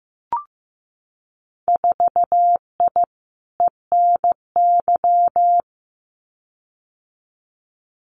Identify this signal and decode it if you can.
Morse code, heard as a keyed tone, about 15 wpm, 700 Hz. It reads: 4I ENY
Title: Winter Field Day Exchanges 15wpm